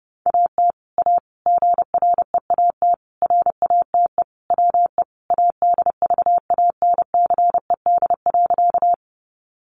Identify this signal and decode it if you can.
Morse code, heard as a keyed tone, about 30 wpm, 700 Hz. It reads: AT A GREAT RATE WE ADVANCED.
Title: Sentences from Top 1200 Words 30wpm